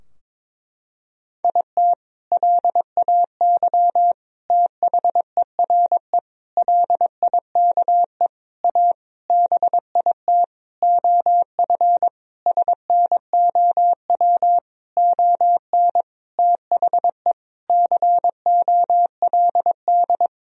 Morse code, heard as a keyed tone, about 22 wpm, 700 Hz. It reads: IT LAY THERE LIKE A BIT OF SNOW ON THE COLD
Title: Sentences from Top 1000 Words - Review Part 3 - 22wpm